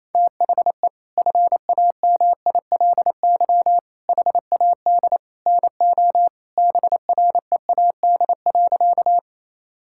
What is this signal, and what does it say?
Morse code, heard as a keyed tone, about 28 wpm, 700 Hz. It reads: THE FAMILY HAD NO BREAD.
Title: Sentences from Top 1600 Words 28wpm